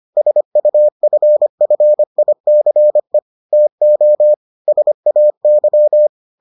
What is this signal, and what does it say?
Morse code, heard as a keyed tone, about 25 wpm, 600 Hz. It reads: SUFFICE TO SAY